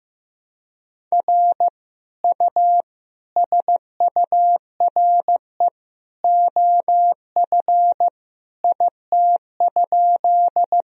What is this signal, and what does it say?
Morse code, heard as a keyed tone, about 15 wpm, 700 Hz. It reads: R U SURE OF IT?
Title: Sentences from Top 300 Words and Abbreviations 15wpm